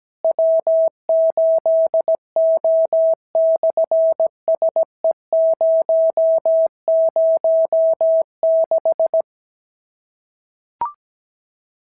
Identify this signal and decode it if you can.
Morse code, heard as a keyed tone, about 17 wpm, 650 Hz. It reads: W8O/SE006
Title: SOTA QSO Practice 17wpm